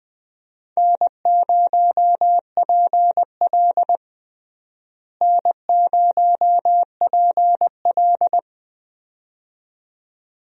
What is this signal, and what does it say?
Morse code, heard as a keyed tone, about 20 wpm, 700 Hz. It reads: N0PL N0PL
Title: USA Call Signs - 4 Characters - Repeated 2x - 20wpm